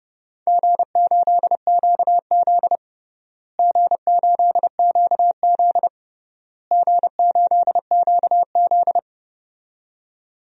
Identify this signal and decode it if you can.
Morse code, heard as a keyed tone, about 30 wpm, 700 Hz. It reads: G8QZ G8QZ G8QZ
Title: DXCC-100 Call Signs - 4 Characters - Repeated 3x - 30wpm